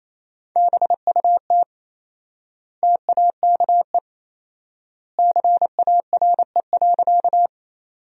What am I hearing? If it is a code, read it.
Morse code, heard as a keyed tone, about 27 wpm, 700 Hz. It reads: BUT TAKE CARE.
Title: 4x Word-Spacing - Sentences from Top 300 Words - 28wpm